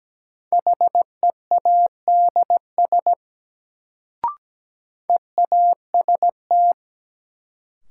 Morse code, heard as a keyed tone, about 17 wpm, 700 Hz. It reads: HEADS EAST
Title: Sentences from Top 1000 Words - Review Part 3 - New Word Drill - 17wpm